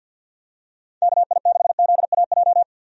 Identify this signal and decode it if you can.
Morse code, heard as a keyed tone, about 47 wpm, 700 Hz. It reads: KI6ZAJ
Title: USA Call Signs - 6 Characters - 50wpm